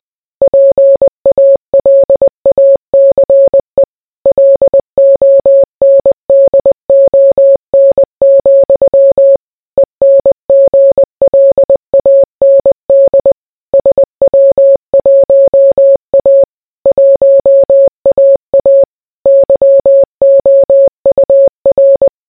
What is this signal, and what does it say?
Morse code, heard as a keyed tone, about 20 wpm, 550 Hz. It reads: PALACE LONDON, ENGLAND SW1A 1AA YOUR